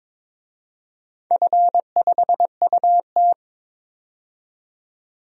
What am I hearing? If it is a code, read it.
Morse code, heard as a keyed tone, about 22 wpm, 700 Hz. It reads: F5UT
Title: DXCC-100 Call Signs - 4 Characters - 22wpm